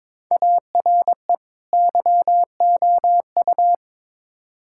Morse code, heard as a keyed tone, about 21 wpm, 700 Hz. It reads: ARE YOU
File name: practice / 22wpm / Build A Phrase - 5-Word Phrases - 22wpm